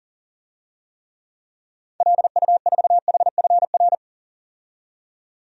Morse code, heard as a keyed tone, about 40 wpm, 700 Hz. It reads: LU4HFR
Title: DXCC-100 Call Signs - 6 Characters - 40wpm